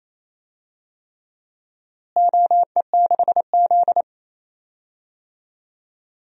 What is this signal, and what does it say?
Morse code, heard as a keyed tone, about 28 wpm, 700 Hz. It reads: OE6Z
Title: DXCC-100 Call Signs - 4 Characters - Repeated 3x - 28wpm